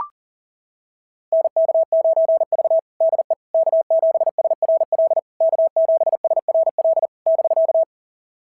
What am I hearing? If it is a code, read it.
Morse code, heard as a keyed tone, about 37 wpm, 660 Hz. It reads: NK9V DE K7SRL K7SRL <BK>
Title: SOTA QSO Practice 40wpm